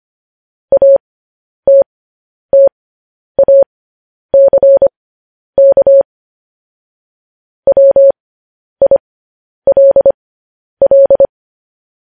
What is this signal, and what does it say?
Morse code, heard as a keyed tone, about 25 wpm, spaced as for 10 wpm, 550 Hz. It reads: ATTACK WILL
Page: news headlines in morse code at 10 wpm